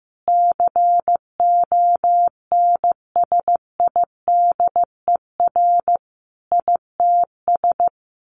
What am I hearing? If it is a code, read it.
Morse code, heard as a keyed tone, about 14 wpm, 700 Hz. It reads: CONSIDER ITS
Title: Sentences from Top 1100 Words 15wpm